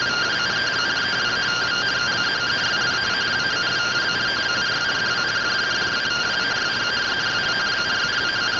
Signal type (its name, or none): two-tone teleprinter signal